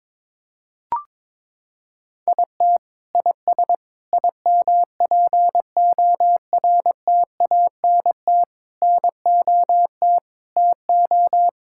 Morse code, heard as a keyed tone, about 22 wpm, 700 Hz. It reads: IT IS IMPORTANT NOT TO
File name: Sentences with 9-Word Phrases - 22wpm